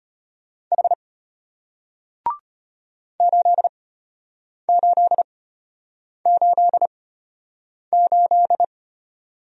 Morse code, heard as a keyed tone, about 33 wpm, spaced as for 8 wpm, 700 Hz. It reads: H 8888